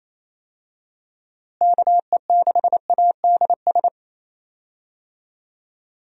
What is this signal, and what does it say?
Morse code, heard as a keyed tone, about 28 wpm, 700 Hz. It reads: KE6ADS
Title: USA Call Signs - 6 Characters - Repeated 2x - 28wpm